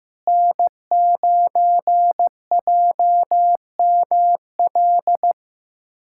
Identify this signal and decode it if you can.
Morse code, heard as a keyed tone, about 16 wpm, 700 Hz. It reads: N9JML
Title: USA Callsign DE Callsign - 5 Characters - 15wpm